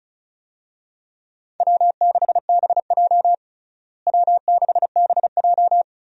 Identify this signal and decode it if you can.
Morse code, heard as a keyed tone, about 34 wpm, 700 Hz. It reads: W6BJ W6BJ
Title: USA Call Signs - 4 Characters - Repeated 2x - 35wpm